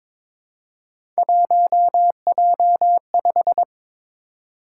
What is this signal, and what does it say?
Morse code, heard as a keyed tone, about 22 wpm, 700 Hz. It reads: 1J5
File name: practice / 22wpm / Three Letter-Number Combinations - 22wpm